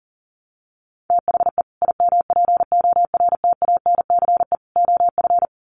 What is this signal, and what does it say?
Morse code, heard as a keyed tone, about 40 wpm, 700 Hz. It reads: THE IMPORTANCE OF
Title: Sets of 3 Words 40wpm